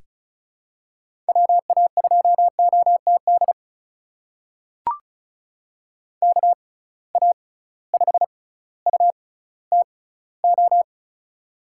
Morse code, heard as a keyed tone, about 36 wpm, spaced as for 18 wpm, 700 Hz. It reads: WA2OTD KA5UTO
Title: USA Call Signs - 6 Characters - 6x Character Spacing - 35wpm